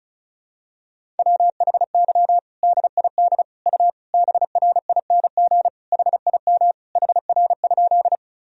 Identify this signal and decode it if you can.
Morse code, heard as a keyed tone, about 36 wpm, 700 Hz. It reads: WHY DID U BRING HIM HR?
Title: Sentences from Top 500 Words and Abbreviations 35wpm